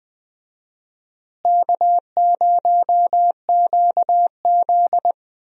Morse code, heard as a keyed tone, about 20 wpm, 700 Hz. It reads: K0QZ
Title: USA Call Signs - 4 Characters - Repeated 3x - 20wpm